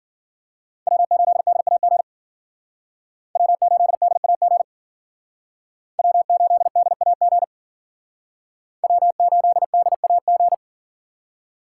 Morse code, heard as a keyed tone, about 50 wpm, 700 Hz. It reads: W8DAG W8DAG W8DAG W8DAG
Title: Speed Racing - USA 1x3 Call Signs 40wpm